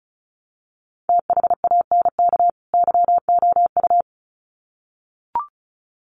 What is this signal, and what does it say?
Morse code, heard as a keyed tone, about 34 wpm, 700 Hz. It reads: THANK YOU
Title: Sets of 2 Words 35wpm